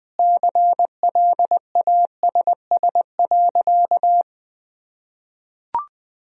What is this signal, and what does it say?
Morse code, heard as a keyed tone, about 21 wpm, 700 Hz. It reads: CLASS.
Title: Sentences from Top 1000 Words - Review Part 4 - 20wpm